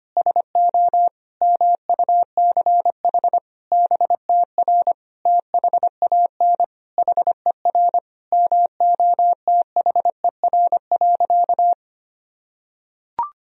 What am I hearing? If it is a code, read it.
Morse code, heard as a keyed tone, about 25 wpm, 700 Hz. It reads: SO MUCH BTR THAN HER MOTHER.
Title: Sentences from Top 400 Words and Abbreviations 25wpm